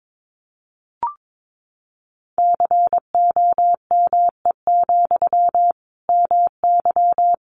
Morse code, heard as a keyed tone, about 22 wpm, 700 Hz. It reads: COME, MY
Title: Sentences from Top 800 Words 22wpm